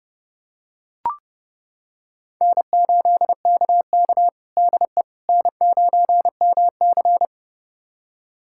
Morse code, heard as a keyed tone, about 32 wpm, 700 Hz. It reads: N8KK DE N9MC
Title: USA Callsign DE Callsign - 4 Characters - 30wpm